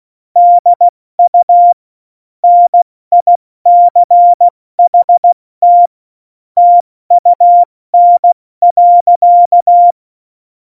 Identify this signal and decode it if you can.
Morse code, heard as a keyed tone, about 16 wpm, 700 Hz. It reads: DU NICHT TUN.
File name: Morsekurs / Witze / Witz 29. August 17, Tempo 70 F80